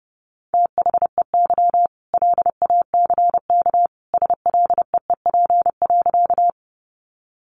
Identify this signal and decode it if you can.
Morse code, heard as a keyed tone, about 30 wpm, 700 Hz. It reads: THEY LACK SLEEP.
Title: Sentences from Top 1100 Words 30wpm